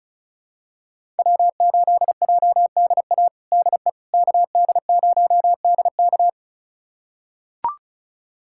Morse code, heard as a keyed tone, about 36 wpm, 700 Hz. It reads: W8JDA DE KD0DK
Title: USA Callsign DE Callsign - 5 Characters - 35wpm